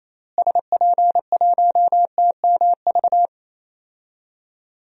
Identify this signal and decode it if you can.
Morse code, heard as a keyed tone, about 27 wpm, 700 Hz. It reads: SP1TMV
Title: DXCC-100 Call Signs - 6 Characters - Repeated 2x - 28wpm